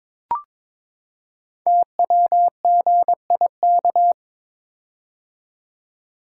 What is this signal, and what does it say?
Morse code, heard as a keyed tone, about 22 wpm, 700 Hz. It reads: TWGIK
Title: Code Groups - 5 Letters - 22wpm